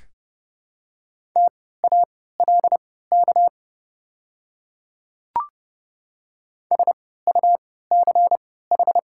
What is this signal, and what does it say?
Morse code, heard as a keyed tone, about 30 wpm, spaced as for 17 wpm, 690 Hz. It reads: TALK SUCH
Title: 3x Character Spacing - Top 500 Words - 30wpm